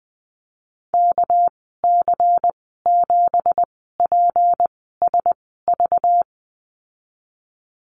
Morse code, heard as a keyed tone, about 20 wpm, spaced as for 15 wpm, 700 Hz. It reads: KC7PSV
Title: 213 Lesson 34g - Number 8 - Call Signs 20wpm